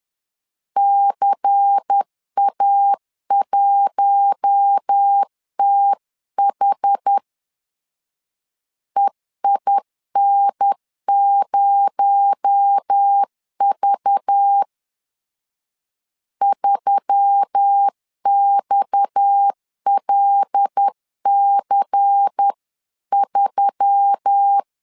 Morse code, heard as a keyed tone, about 10 wpm, 800 Hz. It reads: CA1TH EIN0V 3XLC3
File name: CW / DET / 1 6 1500C S C 10wpm 4252